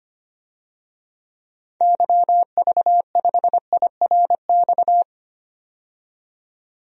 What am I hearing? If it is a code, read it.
Morse code, heard as a keyed tone, about 25 wpm, 700 Hz. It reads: YV5IRX
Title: DXCC-100 Call Signs - 6 Characters - 25wpm